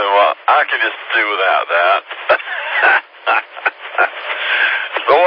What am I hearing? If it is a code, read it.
Emergency Alert System (SAME) tones (header bursts), heard as harsh buzzing bursts, with speech. It is heard from 0.7 to 4.9 s.